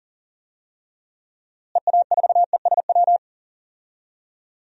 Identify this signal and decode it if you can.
Morse code, heard as a keyed tone, about 40 wpm, 700 Hz. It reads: EA4ESW
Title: DXCC-100 Call Signs - 6 Characters - Repeated 2x - 40wpm